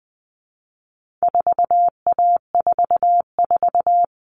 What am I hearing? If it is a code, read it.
Morse code, heard as a keyed tone, about 20 wpm, 700 Hz. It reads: 4A44
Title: All Cut Numbers - 4-Digit 20wpm